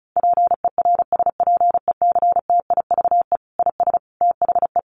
Morse code, heard as a keyed tone, about 34 wpm, 700 Hz. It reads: PERSPECTIVE IS THE
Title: Sentences from Top 1200 Words 35wpm